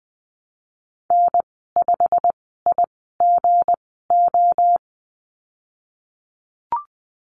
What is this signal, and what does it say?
Morse code, heard as a keyed tone, about 19 wpm, spaced as for 14 wpm, 700 Hz. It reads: N5IGO
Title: 179 Lesson 29g - Slash - Call Signs 20wpm